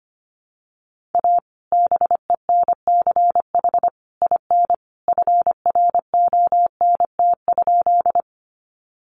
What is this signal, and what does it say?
Morse code, heard as a keyed tone, about 25 wpm, 700 Hz. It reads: A BENCH IN FRONT?
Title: Sentences from Top 1400 Words 25wpm